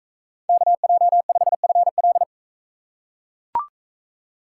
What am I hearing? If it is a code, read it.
Morse code, heard as a keyed tone, about 43 wpm, 700 Hz. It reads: KJ5FL